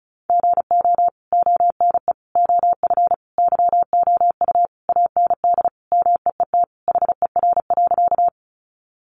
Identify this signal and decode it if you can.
Morse code, heard as a keyed tone, about 34 wpm, 700 Hz. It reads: GO ONE OF YOU AND MEET HER.
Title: Sentences from Top 1100 Words 35wpm